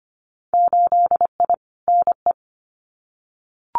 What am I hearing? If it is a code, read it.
Morse code, heard as a keyed tone, about 25 wpm, 700 Hz. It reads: 8I NE
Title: Winter Field Day Exchanges Encore 25wpm